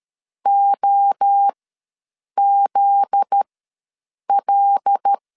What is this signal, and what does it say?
Morse code, heard as a keyed tone, about 13 wpm, spaced as for 7 wpm, 800 Hz. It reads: OZL
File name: LEZIONI / 9 28 ws 12wpm 3354